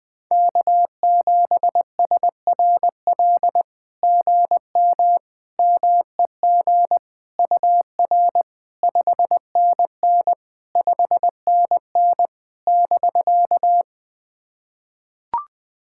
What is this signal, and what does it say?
Morse code, heard as a keyed tone, about 20 wpm, 700 Hz. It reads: K7SRL GM MEG UR 5NN 5NN <BK>
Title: SOTA QSO Practice 20wpm